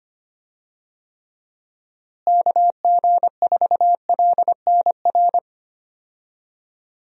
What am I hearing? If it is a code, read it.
Morse code, heard as a keyed tone, about 25 wpm, 700 Hz. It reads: KG4LNR
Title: USA Call Signs - 6 Characters - Repeated 3x - 25wpm